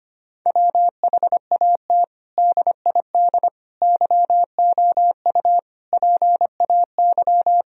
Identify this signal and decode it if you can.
Morse code, heard as a keyed tone, about 25 wpm, 700 Hz. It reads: WHAT DID YOU PAY
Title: Sentences from Top 1000 Words - Review Part 3 - 25wpm